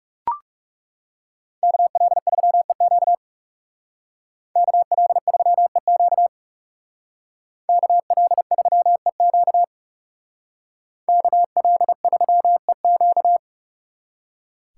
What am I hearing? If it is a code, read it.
Morse code, heard as a keyed tone, about 39 wpm, 700 Hz. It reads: KL3EQ KL3EQ KL3EQ KL3EQ